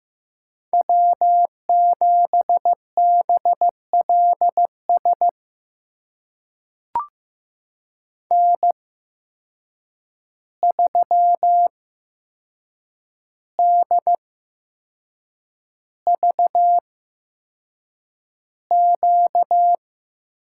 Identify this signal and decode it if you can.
Morse code, heard as a keyed tone, about 15 wpm, spaced as for 6 wpm, 700 Hz. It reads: W7BLS N3DVQ
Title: USA Call Signs - 5 Characters - 8x Character Spacing - 15wpm